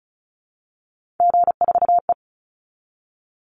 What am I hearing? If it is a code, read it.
Morse code, heard as a keyed tone, about 34 wpm, 700 Hz. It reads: G4E
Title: Call Signs Repeated 2x - 35wpm